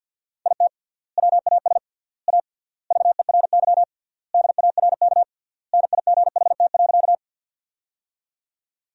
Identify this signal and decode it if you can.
Morse code, heard as a keyed tone, about 52 wpm, 700 Hz. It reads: IT WAS A VERY DARK NIGHT.
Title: 3x Word-Spacing - Sentences from Top 500 Words - 50wpm